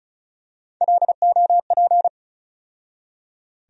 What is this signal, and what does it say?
Morse code, heard as a keyed tone, about 36 wpm, 700 Hz. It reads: LOP
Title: USA Call Sign Postfixes - 3 Letters 35wpm